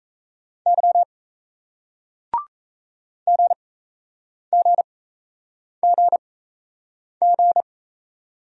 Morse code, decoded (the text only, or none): Y GGGG